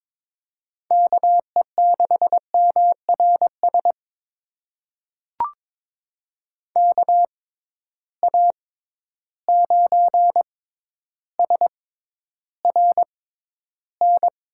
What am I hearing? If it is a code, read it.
Morse code, heard as a keyed tone, about 22 wpm, spaced as for 11 wpm, 700 Hz. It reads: KE6MRS KA9SRN
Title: USA Call Signs - 6 Characters - 6x Character Spacing - 22wpm